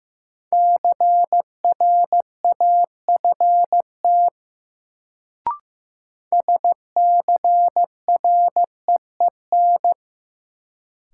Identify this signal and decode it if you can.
Morse code, heard as a keyed tone, about 15 wpm, 700 Hz. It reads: CRAFT SCREEN
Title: Sentences from Top 1000 Words - Review Part 6 - New Word Drill - 15wpm